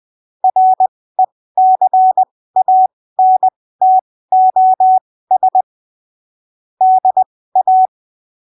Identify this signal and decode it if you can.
Morse code, heard as a keyed tone, about 20 wpm, spaced as for 15 wpm, 750 Hz. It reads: RECANTOS DA
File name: arquivos / texto / texto 29 20ppm